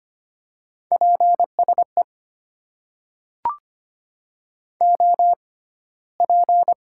Morse code, heard as a keyed tone, about 25 wpm, spaced as for 15 wpm, 700 Hz. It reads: PSE OP
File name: Top 90 QSO Elements - 6x Character Spacing - 25wpm